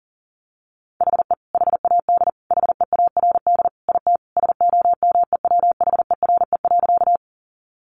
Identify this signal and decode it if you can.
Morse code, heard as a keyed tone, about 40 wpm, 700 Hz. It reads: HE HAD HEARD IT SOMEWHERE.
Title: Sentences from Top 1400 Words 40wpm